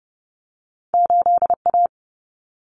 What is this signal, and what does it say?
Morse code, heard as a keyed tone, about 30 wpm, 700 Hz. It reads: 8A
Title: Common Cut Numbers - 2-Digit 30wpm